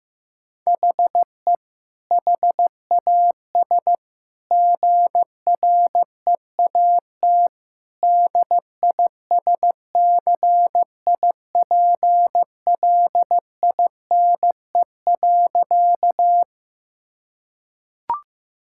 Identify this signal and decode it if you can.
Morse code, heard as a keyed tone, about 15 wpm, 700 Hz. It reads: HE HAS GREAT DISCIPLINE.
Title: Sentences from Top 1000 Words - Review Part 6 - 15wpm